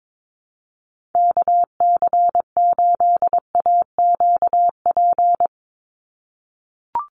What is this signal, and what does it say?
Morse code, heard as a keyed tone, about 22 wpm, 700 Hz. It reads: KC8AQP
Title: USA Call Signs 22wpm